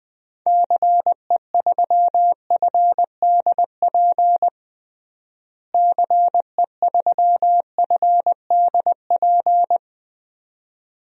Morse code, heard as a keyed tone, about 20 wpm, 700 Hz. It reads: CE3FDP CE3FDP
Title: DXCC-100 Call Signs - 6 Characters - Repeated 2x - 20wpm